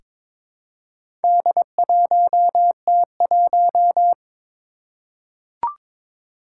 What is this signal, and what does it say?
Morse code, heard as a keyed tone, about 22 wpm, 700 Hz. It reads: D1T1
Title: Four Challenging Letter-Number Combinations 22wpm